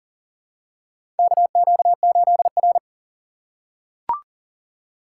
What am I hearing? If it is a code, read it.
Morse code, heard as a keyed tone, about 40 wpm, 700 Hz. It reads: KQ8R
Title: USA Call Signs - 2x Character Spacing - 40wpm